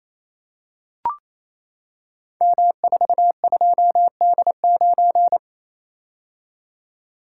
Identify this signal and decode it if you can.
Morse code, heard as a keyed tone, about 27 wpm, 700 Hz. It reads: M42D9
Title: Code Groups - 5 Letter:Numbers - 28wpm